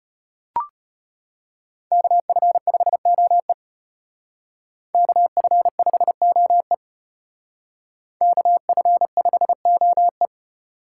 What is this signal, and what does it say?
Morse code, heard as a keyed tone, about 32 wpm, 700 Hz. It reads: KF5OE KF5OE KF5OE